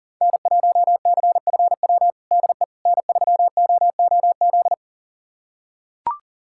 Morse code, heard as a keyed tone, about 40 wpm, 700 Hz. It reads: N1CFW DE N3OOZ
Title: USA Callsign DE Callsign 40wpm